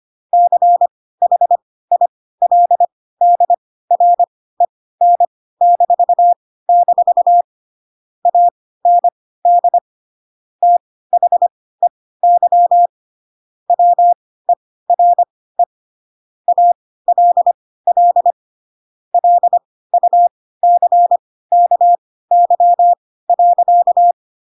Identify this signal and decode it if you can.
Morse code, heard as a keyed tone, about 25 wpm, spaced as for 16 wpm, 700 Hz. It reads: CHILDREN-- AND THEY WERE ALL LUCKY.
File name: Pinocchio / Pinocchio ch3 10 25 ik1hge